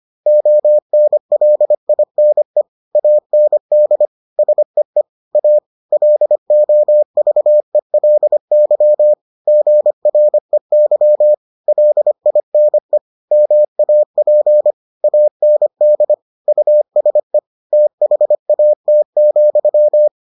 Morse code, heard as a keyed tone, about 25 wpm, 600 Hz. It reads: ONLINE AND SEE A LOVELY GREY LINE MAP AND USE THAT,